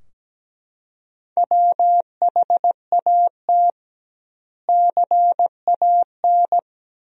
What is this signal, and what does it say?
Morse code, heard as a keyed tone, about 17 wpm, 700 Hz. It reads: WHAT CAN
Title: 2x Word-Spacing - Sentences from Top 100 Words - 17wpm